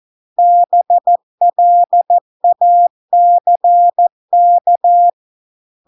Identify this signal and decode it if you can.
Morse code, heard as a keyed tone, about 14 wpm, 700 Hz. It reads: BLACK